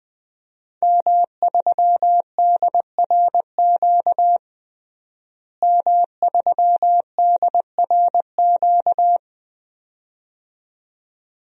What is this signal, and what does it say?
Morse code, heard as a keyed tone, about 20 wpm, 700 Hz. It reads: M3DRQ M3DRQ
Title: DXCC-100 Call Signs - 5 Characters - Repeated 2x - 20wpm